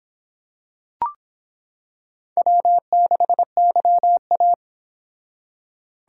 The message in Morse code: W6YA

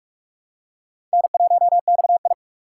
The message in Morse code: N1XI

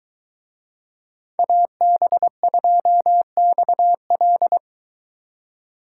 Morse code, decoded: AB2XL

